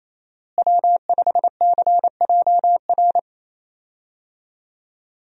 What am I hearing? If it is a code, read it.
Morse code, heard as a keyed tone, about 28 wpm, 700 Hz. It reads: W5CJR
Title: USA Call Signs - 5 Characters - Repeated 3x - 28wpm